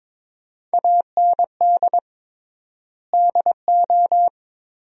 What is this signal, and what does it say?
Morse code, heard as a keyed tone, about 22 wpm, 700 Hz. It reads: AND DO
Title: 3x Word-Spacing - Sentences from Top 100 Words - 22wpm